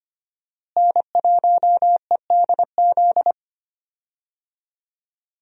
Morse code, heard as a keyed tone, about 25 wpm, 700 Hz. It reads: N1EDZ
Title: USA Call Signs - 5 Characters - Repeated 3x - 25wpm